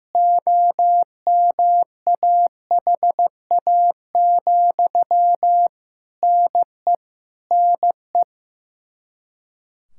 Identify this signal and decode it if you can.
Morse code, heard as a keyed tone, about 16 wpm, 700 Hz. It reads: OMAHA, NE NE
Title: Practice QSO - Exchange 1 - Beginning - 15wpm